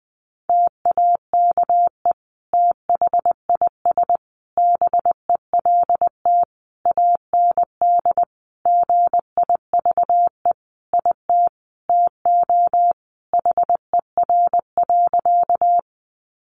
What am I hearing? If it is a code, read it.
Morse code, heard as a keyed tone, about 19 wpm, 700 Hz. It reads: TAKE THIS BELT AND GIVE IT TO HER.
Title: Sentences from Top 1400 Words 20wpm